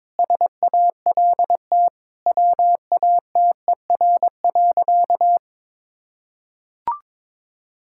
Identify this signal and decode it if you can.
Morse code, heard as a keyed tone, about 22 wpm, 700 Hz. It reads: SALT WATER.
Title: Sentences from Top 1000 Words - Review Part 5 - 22wpm